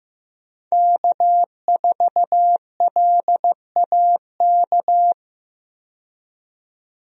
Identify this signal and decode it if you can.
Morse code, heard as a keyed tone, about 15 wpm, 700 Hz. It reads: K4LAK
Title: USA Call Signs - 5 Characters - 15wpm